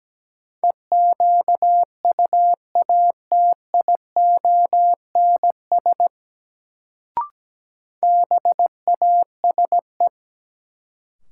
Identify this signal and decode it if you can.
Morse code, heard as a keyed tone, about 17 wpm, 700 Hz. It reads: EQUATIONS BASE
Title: Sets of 5 Words - New Word Drill - 17wpm